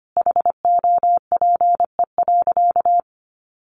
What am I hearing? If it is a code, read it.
Morse code, heard as a keyed tone, about 25 wpm, 700 Hz. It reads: HOPE.
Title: Sentences from Top 1400 Words 25wpm